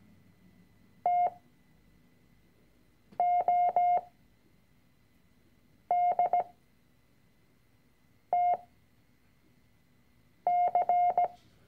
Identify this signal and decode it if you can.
Morse code, heard as a keyed tone, about 17 wpm, spaced as for 4 wpm, 710 Hz. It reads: TODTC
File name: CW-Online-Kurs / Anfänger / Lektion 006